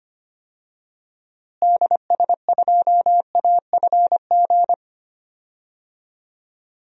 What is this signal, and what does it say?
Morse code, heard as a keyed tone, about 25 wpm, 700 Hz. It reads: DS2AFG
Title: DXCC-100 Call Signs - 6 Characters - Repeated 2x - 25wpm